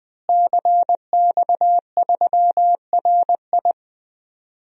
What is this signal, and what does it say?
Morse code, heard as a keyed tone, about 20 wpm, 700 Hz. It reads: CX3RI
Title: DXCC-100 Call Signs - 5 Characters - Repeated 3x - 20wpm